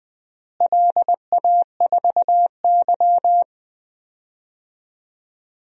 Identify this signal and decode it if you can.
Morse code, heard as a keyed tone, about 21 wpm, 700 Hz. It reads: LA4Y
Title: DXCC-100 Call Signs - 4 Characters - Repeated 3x - 20wpm